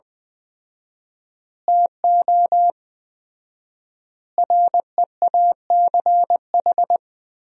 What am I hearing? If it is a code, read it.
Morse code, heard as a keyed tone, about 21 wpm, 700 Hz. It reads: TO REACH
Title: 4x Word-Spacing - Sentences from Top 400 Words - 20wpm